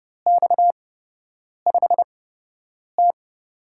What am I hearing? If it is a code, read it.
Morse code, heard as a keyed tone, about 31 wpm, spaced as for 8 wpm, 700 Hz. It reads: X5T
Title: USA Call Signs - 4 Characters - 8x Character Spacing - 30wpm